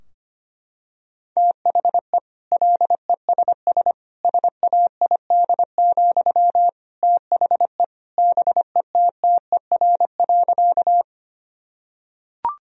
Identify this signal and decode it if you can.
Morse code, heard as a keyed tone, about 25 wpm, 700 Hz. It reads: THE LESS SAID, THE BETTER.
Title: Sentences from Top 1000 Words - Review Part 2 - 25wpm